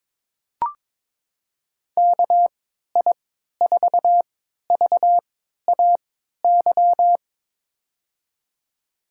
Morse code, heard as a keyed tone, about 22 wpm, spaced as for 13 wpm, 700 Hz. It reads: KI4VAY
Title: USA Call Signs - 6 Characters - 3x Character Spacing - 22wpm